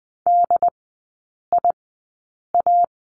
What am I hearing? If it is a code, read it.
Morse code, heard as a keyed tone, about 20 wpm, spaced as for 8 wpm, 700 Hz. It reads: DIA
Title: 133 Lesson 22e - Question Mark - Three Characters 20wpm